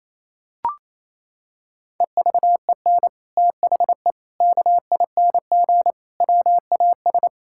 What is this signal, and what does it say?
Morse code, heard as a keyed tone, about 27 wpm, 700 Hz. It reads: EVEN THE KING WAS
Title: Sentences from Top 1000 Words - Review Part 6 - 28wpm